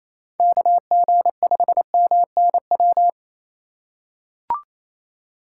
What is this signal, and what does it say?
Morse code, heard as a keyed tone, about 28 wpm, 690 Hz. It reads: KG5MNW
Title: USA Call Signs - 6 Characters - 2x Character Spacing - 28wpm